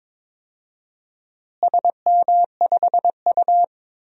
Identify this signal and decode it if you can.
Morse code, heard as a keyed tone, about 22 wpm, 700 Hz. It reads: SM5U
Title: DXCC-100 Call Signs - 4 Characters - Repeated 3x - 22wpm